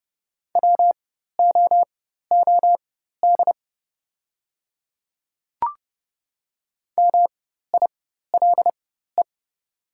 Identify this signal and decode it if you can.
Morse code, heard as a keyed tone, about 30 wpm, spaced as for 14 wpm, 700 Hz. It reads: WOOD MILE
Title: 4x Character Spacing - Top 300 Words - 30wpm